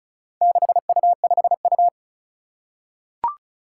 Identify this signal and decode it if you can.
Morse code, heard as a keyed tone, about 36 wpm, 700 Hz. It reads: 6U5U
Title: Four Challenging Letter-Number Combinations 35wpm